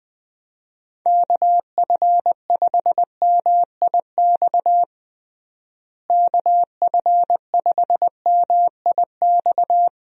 Morse code, heard as a keyed tone, about 20 wpm, 700 Hz. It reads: KF5MIX KF5MIX
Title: USA Call Signs - 6 Characters - Repeated 3x - 20wpm